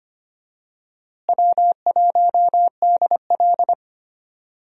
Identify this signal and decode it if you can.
Morse code, heard as a keyed tone, about 25 wpm, 700 Hz. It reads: W1DL
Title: USA Call Signs - 4 Characters - Repeated 2x - 25wpm